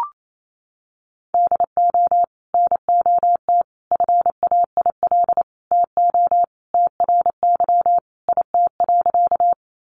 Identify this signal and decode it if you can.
Morse code, heard as a keyed tone, about 28 wpm, 700 Hz. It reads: DO NOT FAIL TO TRY IT.
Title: Sentences from Top 1100 Words 28wpm